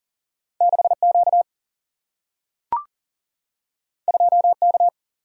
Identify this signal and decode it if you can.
Morse code, heard as a keyed tone, about 39 wpm, 700 Hz. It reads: 6Q 2K
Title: Single-Letter Single-Number 40wpm